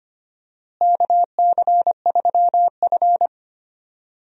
Morse code, heard as a keyed tone, about 25 wpm, 700 Hz. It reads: KC3F
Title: USA Call Signs - 4 Characters - Repeated 3x - 25wpm